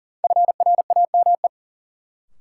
Morse code, heard as a keyed tone, about 40 wpm, 700 Hz. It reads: FRAME
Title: Sentences from Top 1000 Words - Review Part 6 - New Word Drill - 40wpm